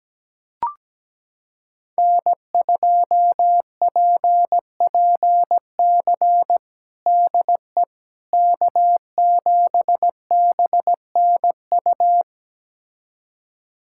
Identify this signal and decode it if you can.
Morse code, heard as a keyed tone, about 17 wpm, 700 Hz. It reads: N2PPC DE K7BNU
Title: USA Callsign DE Callsign - 5 Characters - 17wpm